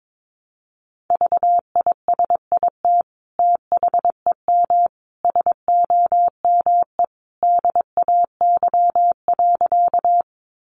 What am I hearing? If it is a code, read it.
Morse code, heard as a keyed tone, about 22 wpm, 700 Hz. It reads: VISIT THEM SOME DAY.
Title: Sentences from Top 900 Words 22wpm